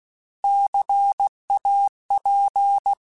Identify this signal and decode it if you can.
Morse code, heard as a keyed tone, about 16 wpm, 800 Hz. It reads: CAP
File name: Alphabetical / Alphabetical mp3s / Cs @15